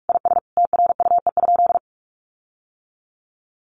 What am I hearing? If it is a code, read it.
Morse code, heard as a keyed tone, about 44 wpm, 700 Hz. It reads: IS TRUE?
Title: Sentences from Top 1400 Words 45wpm